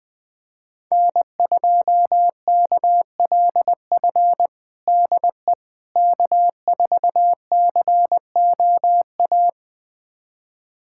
Morse code, heard as a keyed tone, about 21 wpm, 700 Hz. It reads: N2KLF DE K4COA
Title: USA Callsign DE Callsign - 5 Characters - 20wpm